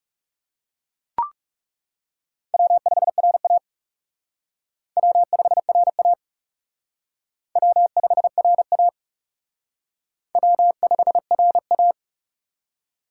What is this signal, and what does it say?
Morse code, heard as a keyed tone, about 37 wpm, 700 Hz. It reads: W5RA W5RA W5RA W5RA
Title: Speed Racing - USA 1x2 Call Signs 30wpm